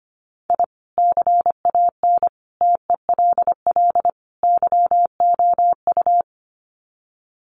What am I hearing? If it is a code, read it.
Morse code, heard as a keyed tone, about 25 wpm, 700 Hz. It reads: I CAN TELL YOU
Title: Sets of 4 Words 25wpm